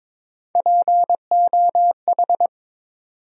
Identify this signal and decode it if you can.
Morse code, heard as a keyed tone, about 22 wpm, 700 Hz. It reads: POH
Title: USA Call Sign Postfixes - 3 Letters 22wpm